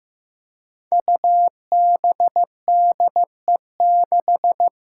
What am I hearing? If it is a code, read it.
Morse code, heard as a keyed tone, about 15 wpm, 700 Hz. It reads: UBDE6
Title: Five Challenging Letter-Number Combinations 15wpm